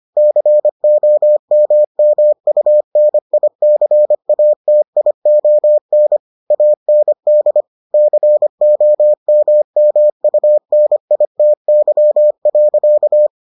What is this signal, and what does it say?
Morse code, heard as a keyed tone, about 25 wpm, 600 Hz. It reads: COMMUNICATION AND COMMUNITY.